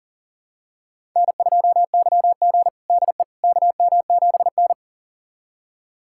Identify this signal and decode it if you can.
Morse code, heard as a keyed tone, about 37 wpm, 700 Hz. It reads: N2YG DE KM7N
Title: USA Callsign DE Callsign - 4 Characters - 40wpm